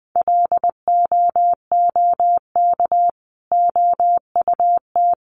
Morse code, heard as a keyed tone, about 20 wpm, 700 Hz. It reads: LOOK OUT
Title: Sentences from Top 900 Words 20wpm